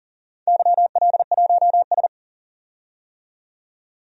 Morse code, heard as a keyed tone, about 44 wpm, 690 Hz. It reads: YL1S
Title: DXCC-100 Call Signs - 4 Characters - 40wpm